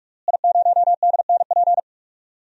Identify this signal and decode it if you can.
Morse code, heard as a keyed tone, about 46 wpm, 700 Hz. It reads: I0DNP
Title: DXCC-100 Call Signs - 5 Characters - Repeated 3x - 45wpm